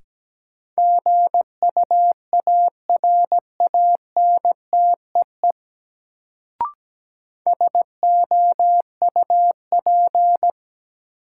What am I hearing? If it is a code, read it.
Morse code, heard as a keyed tone, about 17 wpm, 700 Hz. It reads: GUARANTEE SOUP
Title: Sentences from Top 1600 Words - New Word Drill Alt - 17wpm